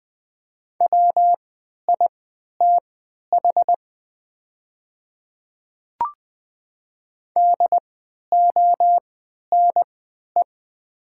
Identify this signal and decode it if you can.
Morse code, heard as a keyed tone, about 20 wpm, spaced as for 11 wpm, 700 Hz. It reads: WITH DONE
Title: 3x Character Spacing - Top 500 Words - 20wpm